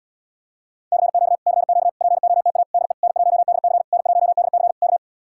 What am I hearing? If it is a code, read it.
Morse code, heard as a keyed tone, about 77 wpm, 700 Hz. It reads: CQ CQ POTA DE N1DC N1DC K